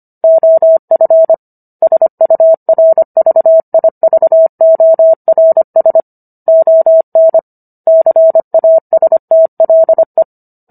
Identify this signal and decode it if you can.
Morse code, heard as a keyed tone, about 25 wpm, 650 Hz. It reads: OF SURVIVORS ON CASTLE